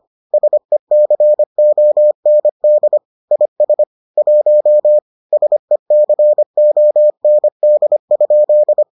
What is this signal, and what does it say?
Morse code, heard as a keyed tone, about 25 wpm, 600 Hz. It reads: SECOND IS 1 SECOND?